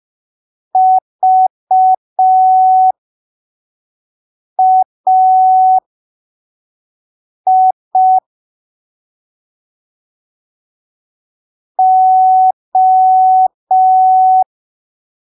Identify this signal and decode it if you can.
Morse code, heard as a keyed tone, about 5 wpm, spaced as for 3 wpm, 750 Hz. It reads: VAI O
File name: arquivos / texto / texto 13 05ppm